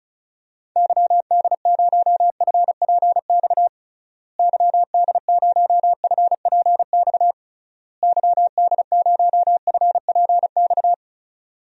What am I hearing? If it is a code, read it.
Morse code, heard as a keyed tone, about 36 wpm, 700 Hz. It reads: YD0FPX YD0FPX YD0FPX
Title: DXCC-100 Call Signs - 6 Characters - Repeated 3x - 35wpm